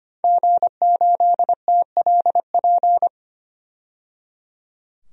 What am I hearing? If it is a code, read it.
Morse code, heard as a keyed tone, about 25 wpm, 700 Hz. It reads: G8TLP
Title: DXCC-100 Call Signs - 5 Characters - 25wpm